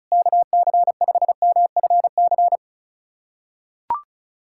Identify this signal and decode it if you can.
Morse code, heard as a keyed tone, about 36 wpm, 700 Hz. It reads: KC5MFC
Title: USA Callsign DE Callsign - 6 Characters - 35wpm